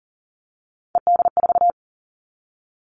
Morse code, heard as a keyed tone, about 40 wpm, 700 Hz. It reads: ED4
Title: All Cut Numbers - 3-Digit 40wpm